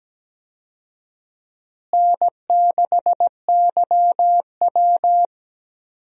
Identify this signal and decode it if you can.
Morse code, heard as a keyed tone, about 17 wpm, 700 Hz. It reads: N6YW
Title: USA Call Signs - 4 Characters - Repeated 2x - 17wpm